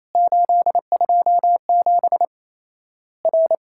Morse code, heard as a keyed tone, about 28 wpm, 700 Hz. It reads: 827 R
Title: Worked All Europe DX Contest - QTC Exchange - 28wpm